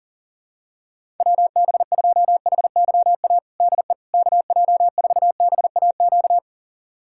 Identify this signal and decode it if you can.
Morse code, heard as a keyed tone, about 37 wpm, 700 Hz. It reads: WB2HYA DE KJ4BAQ
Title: USA Callsign DE Callsign - 6 Characters - 40wpm